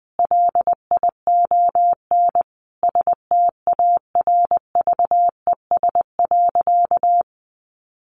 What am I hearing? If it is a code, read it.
Morse code, heard as a keyed tone, about 20 wpm, 700 Hz. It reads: LION STARVES.
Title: A Sound Of Thunder 20wpm